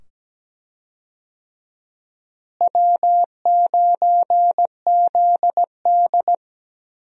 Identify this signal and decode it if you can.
Morse code, heard as a keyed tone, about 17 wpm, 700 Hz. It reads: W9ZD
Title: USA Call Signs - 4 Characters - Repeated 2x - 17wpm